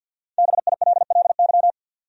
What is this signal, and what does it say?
Morse code, heard as a keyed tone, about 50 wpm, 700 Hz. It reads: BILLY
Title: USA Common Names - 50wpm